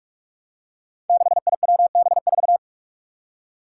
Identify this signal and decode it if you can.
Morse code, heard as a keyed tone, about 42 wpm, 700 Hz. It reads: 6IWB4
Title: Five Challenging Letter-Number Combinations 45wpm